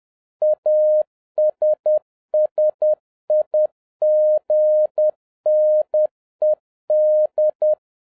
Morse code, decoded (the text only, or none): ASSIGNED